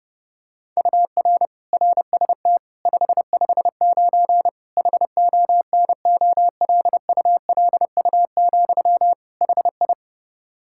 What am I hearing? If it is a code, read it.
Morse code, heard as a keyed tone, about 32 wpm, 700 Hz. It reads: UR RST 559 HONOLULU, HI
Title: Practice QSO - Exchange 1 - Beginning - 30wpm